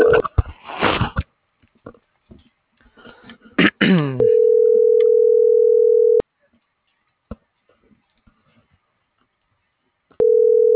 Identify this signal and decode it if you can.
Ringback tone (North America), 440 + 480 Hz.